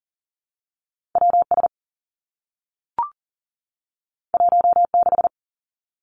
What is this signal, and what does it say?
Morse code, heard as a keyed tone, about 38 wpm, 700 Hz. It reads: W S 1 6
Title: Two Challenging Letter-Number Combinations 40wpm